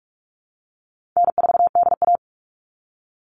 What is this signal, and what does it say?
Morse code, heard as a keyed tone, about 44 wpm, 700 Hz. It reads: N4DA
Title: NA SOTA Chaser Callsigns 45wpm